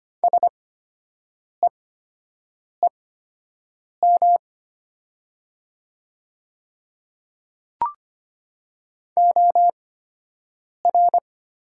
Morse code, heard as a keyed tone, about 25 wpm, spaced as for 7 wpm, 700 Hz. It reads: SEEM OR
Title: 8x Character Spacing - Top 300 Words - 25wpm